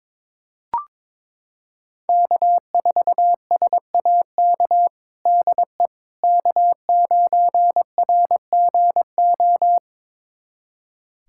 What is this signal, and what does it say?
Morse code, heard as a keyed tone, about 22 wpm, 700 Hz. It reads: K4SAK DE K9RGO
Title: USA Callsign DE Callsign - 5 Characters - 22wpm